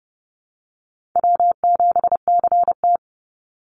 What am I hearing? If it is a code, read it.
Morse code, heard as a keyed tone, about 30 wpm, 700 Hz. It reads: W7CT
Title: Easy Call Signs Repeated 2x - 30wpm